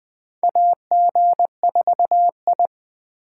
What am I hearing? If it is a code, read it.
Morse code, heard as a keyed tone, about 20 wpm, 700 Hz. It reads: AG4I
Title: USA Call Signs - 4 Characters - 4x Character Spacing - 20wpm